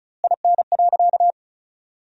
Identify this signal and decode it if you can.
Morse code, heard as a keyed tone, about 36 wpm, 700 Hz. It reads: IN.
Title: 2x Word-Spacing - Sentences from Top 400 Words - 35wpm